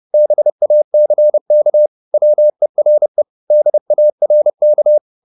Morse code, heard as a keyed tone, about 30 wpm, 600 Hz. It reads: BACK WERE DARK
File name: the old man and the sea / the old man and the sea 1 30wpm